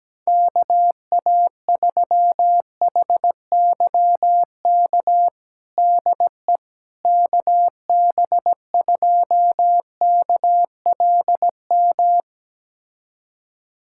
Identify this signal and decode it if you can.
Morse code, heard as a keyed tone, about 17 wpm, 700 Hz. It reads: KA3HYK DE KB2KLM
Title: USA Callsign DE Callsign - 6 Characters - 17wpm